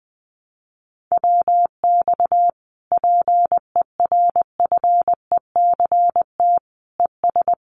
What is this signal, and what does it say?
Morse code, heard as a keyed tone, about 20 wpm, 700 Hz. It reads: WX PERFECT ES
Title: Common QSO Phrases Encore 20wpm